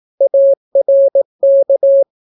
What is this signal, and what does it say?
Morse code, heard as a keyed tone, about 18 wpm, 550 Hz. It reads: ARK